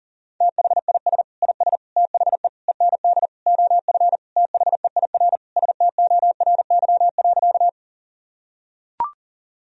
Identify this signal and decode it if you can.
Morse code, heard as a keyed tone, about 40 wpm, 700 Hz. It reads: THIS IS THE END OF THEIR STORY.
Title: Sentences from Top 1000 Words - Review Part 1 - 40wpm